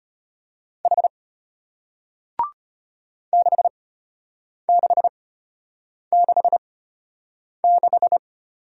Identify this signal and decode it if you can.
Morse code, heard as a keyed tone, about 34 wpm, spaced as for 8 wpm, 700 Hz. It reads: H 6666